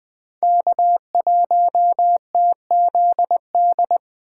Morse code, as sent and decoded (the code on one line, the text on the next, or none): -.- .---- - --.. -..
K1TZD